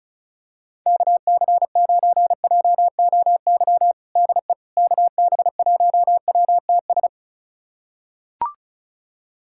Morse code, heard as a keyed tone, about 34 wpm, 690 Hz. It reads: KC9JOY DE KB1WTS
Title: USA Callsign DE Callsign - 6 Characters - 35wpm